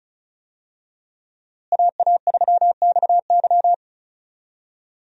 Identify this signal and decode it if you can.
Morse code, heard as a keyed tone, about 34 wpm, 700 Hz. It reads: AA3XY
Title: USA Call Signs - 5 Characters - Repeated 2x - 35wpm